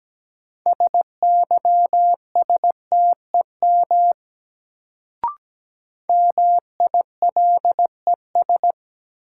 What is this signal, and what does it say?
Morse code, heard as a keyed tone, about 17 wpm, 700 Hz. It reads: SYSTEM MILES
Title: Sentences from Top 1000 Words - Review Part 3 - New Word Drill - 17wpm